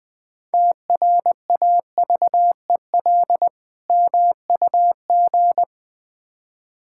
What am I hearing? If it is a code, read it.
Morse code, heard as a keyed tone, about 21 wpm, 700 Hz. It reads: TRAVEL MUG
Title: Difficult Words with Hints - 20wpm